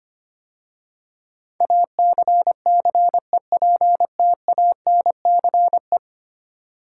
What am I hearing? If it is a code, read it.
Morse code, heard as a keyed tone, about 25 wpm, 700 Hz. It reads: ACCEPTANCE
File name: 10 Letter Words - First 3 Letters Given - 25wpm